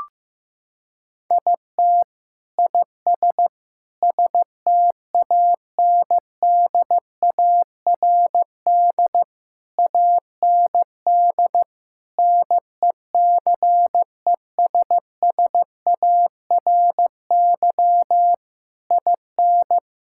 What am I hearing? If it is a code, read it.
Morse code, heard as a keyed tone, about 15 wpm, 700 Hz. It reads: IT IS STANDARD AND NECESSARY IN
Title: Sentences from Top 1000 Words - Review Part 6 - 15wpm